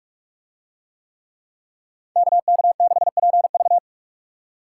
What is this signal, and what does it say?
Morse code, heard as a keyed tone, about 44 wpm, 700 Hz. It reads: KK6PV